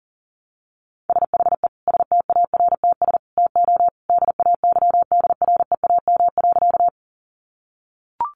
Morse code, heard as a keyed tone, about 40 wpm, 700 Hz. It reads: SHE STARTS TO DAYDREAM.
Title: Easy Phrases 40wpm